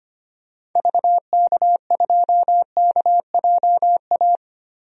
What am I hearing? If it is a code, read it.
Morse code, heard as a keyed tone, about 25 wpm, 700 Hz. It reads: VK2KJA
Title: DXCC-100 Call Signs - 6 Characters - 25wpm